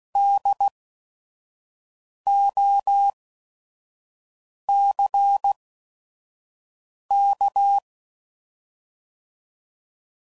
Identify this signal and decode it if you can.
Morse code, heard as a keyed tone, about 16 wpm, spaced as for 5 wpm, 800 Hz. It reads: DOCK
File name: Prefixes / By First Letter / Ds